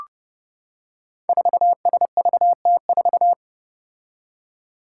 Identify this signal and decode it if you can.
Morse code, heard as a keyed tone, about 30 wpm, 700 Hz. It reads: 4SVT4
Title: Five Challenging Letter-Number Combinations 30wpm